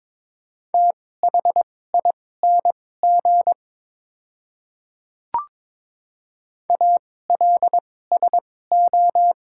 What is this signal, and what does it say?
Morse code, heard as a keyed tone, about 22 wpm, spaced as for 16 wpm, 700 Hz. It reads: THING ALSO
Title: 2x Character Spacing - Top 200 Words - 22wpm